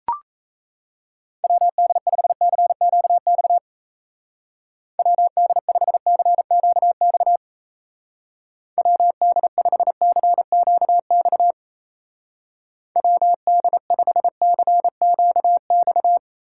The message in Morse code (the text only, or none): WD5CQX WD5CQX WD5CQX WD5CQX